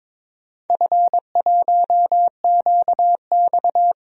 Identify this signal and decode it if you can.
Morse code, heard as a keyed tone, about 22 wpm, 700 Hz. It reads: F1QX
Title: DXCC-100 Call Signs - 4 Characters - 22wpm